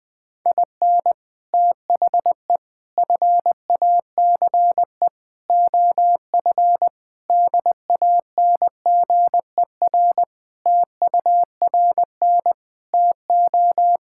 Morse code, heard as a keyed tone, about 20 wpm, 700 Hz. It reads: IN THE FACE OF DANGER TURN TO
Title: Sentences with 3-Word Phrases - 20wpm